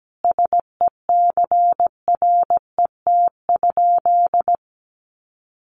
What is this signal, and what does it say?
Morse code, heard as a keyed tone, about 17 wpm, 700 Hz. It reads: SECRET?
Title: Sentences from Top 1200 Words 17wpm